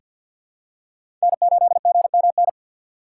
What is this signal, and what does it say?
Morse code, heard as a keyed tone, about 47 wpm, 700 Hz. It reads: N8GMN
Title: USA Call Signs - 6x Character Spacing - 50wpm